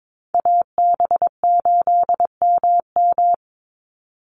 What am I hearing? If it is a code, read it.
Morse code, heard as a keyed tone, about 22 wpm, 700 Hz. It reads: AB8MM
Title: USA Call Signs 22wpm